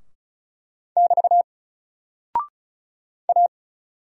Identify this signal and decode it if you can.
Morse code, heard as a keyed tone, about 36 wpm, 700 Hz. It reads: <BT> A
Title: Single Character (Letter:Number:Punctuation:Prosign) Alt 35wpm